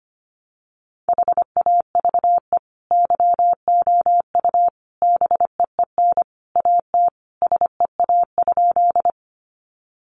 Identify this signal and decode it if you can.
Morse code, heard as a keyed tone, about 25 wpm, 700 Hz. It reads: HAVE YOU BEEN AT SEA?
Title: Sentences from Top 300 Words 25wpm